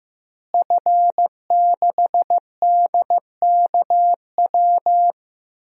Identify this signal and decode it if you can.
Morse code, heard as a keyed tone, about 15 wpm, 700 Hz. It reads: F6DKW
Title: DXCC-100 Call Signs - 5 Characters - Repeated 3x - 15wpm